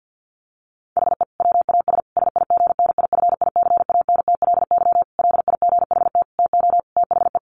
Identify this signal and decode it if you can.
Morse code, heard as a keyed tone, about 50 wpm, 700 Hz. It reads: HE WAS SIGNIFICANTLY LIGHT TO THE
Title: Sentences from Top 1500 Words 50wpm